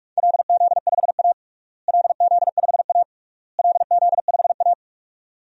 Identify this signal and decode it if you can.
Morse code, heard as a keyed tone, about 42 wpm, 700 Hz. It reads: LZ5A LZ5A LZ5A
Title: DXCC-100 Call Signs - 4 Characters - Repeated 3x - 45wpm